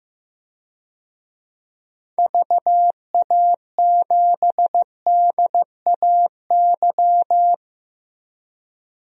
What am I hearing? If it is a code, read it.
Morse code, heard as a keyed tone, about 15 wpm, 700 Hz. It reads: VA7DAY
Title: DXCC-100 Call Signs - 6 Characters - Repeated 3x - 15wpm